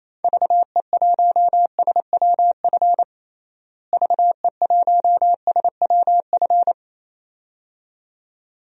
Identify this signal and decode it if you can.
Morse code, heard as a keyed tone, about 28 wpm, 700 Hz. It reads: VE1SWF VE1SWF
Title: DXCC-100 Call Signs - 6 Characters - Repeated 2x - 28wpm